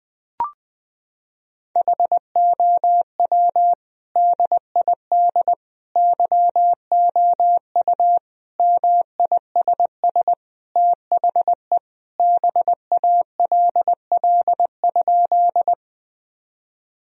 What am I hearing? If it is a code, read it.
Morse code, heard as a keyed tone, about 20 wpm, 700 Hz. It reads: HOW DID YOU MISS THE BALL?
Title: Sentences from Top 1000 Words - Review Part 3 - 20wpm